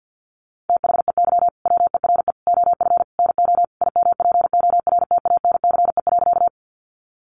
Morse code, heard as a keyed tone, about 50 wpm, 700 Hz. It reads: THEY WERE OF NO IMPORTANCE.
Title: Sentences from Top 1200 Words 50wpm